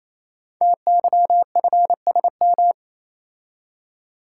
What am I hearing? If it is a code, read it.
Morse code, heard as a keyed tone, about 29 wpm, 700 Hz. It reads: TYFSM
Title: Code Groups - 5 Letters - 28wpm